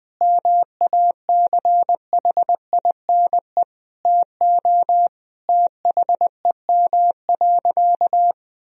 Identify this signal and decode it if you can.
Morse code, heard as a keyed tone, about 20 wpm, 700 Hz. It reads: MACHINE TO THEM.
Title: Sentences from Top 1000 Words - Review Part 2 - 20wpm